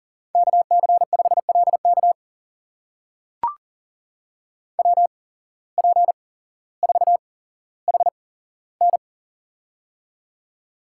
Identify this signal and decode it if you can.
Morse code, heard as a keyed tone, about 40 wpm, spaced as for 17 wpm, 700 Hz. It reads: KC5LK WP4HN
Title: USA Call Signs - 5 Characters - 8x Character Spacing - 40wpm